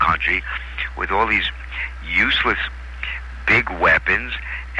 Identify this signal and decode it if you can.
Emergency Alert System (SAME) tones, heard as harsh buzzing bursts, with speech. It is heard from 0.2 to 3.6 s.